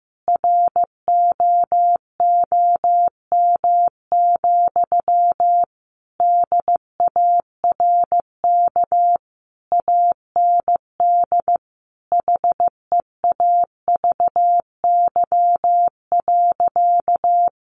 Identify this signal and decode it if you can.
Morse code, heard as a keyed tone, about 15 wpm, 700 Hz. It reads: ROOM, DARK AND HEAVY.
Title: Sentences from Top 1400 Words 15wpm